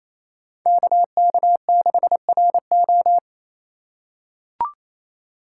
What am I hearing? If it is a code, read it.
Morse code, heard as a keyed tone, about 27 wpm, 700 Hz. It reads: KK6RO
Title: USA Call Signs - 5 Characters - 2x Character Spacing - 28wpm